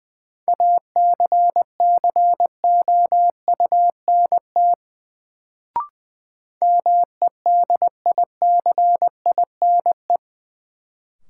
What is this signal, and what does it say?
Morse code, heard as a keyed tone, about 20 wpm, 700 Hz. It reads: ACCOUNT MEDICINE
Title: Sentences from Top 900 Words - New Word Drill - 20wpm